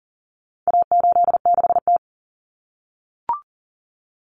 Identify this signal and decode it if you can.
Morse code, heard as a keyed tone, about 40 wpm, 700 Hz. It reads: A86T
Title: Common Cut Numbers - 4-Digit 40wpm